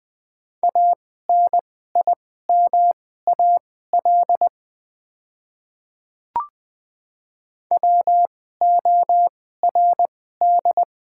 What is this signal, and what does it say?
Morse code, heard as a keyed tone, about 20 wpm, spaced as for 15 wpm, 700 Hz. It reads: ANIMAL WORD
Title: 2x Character Spacing - Top 200 Words - 20wpm